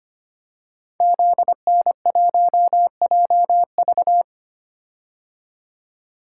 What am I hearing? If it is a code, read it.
Morse code, heard as a keyed tone, about 25 wpm, 700 Hz. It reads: ZN1JV
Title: Code Groups - 5 Letter:Numbers - 25wpm